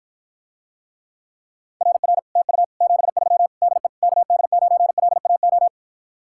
Morse code, heard as a keyed tone, about 54 wpm, 700 Hz. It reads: RR TU 73 DE KD9LAO